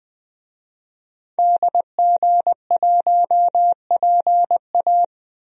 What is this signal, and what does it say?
Morse code, heard as a keyed tone, about 21 wpm, 700 Hz. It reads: DG1PA
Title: DXCC-100 Call Signs - 5 Characters - Repeated 2x - 20wpm